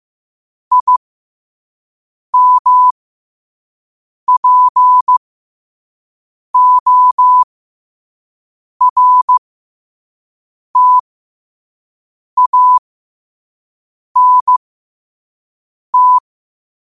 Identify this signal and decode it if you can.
Morse code, heard as a keyed tone, about 15 wpm, spaced as for 5 wpm, 1000 Hz. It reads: IMPORTANT
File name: qotdfiles / 5wpm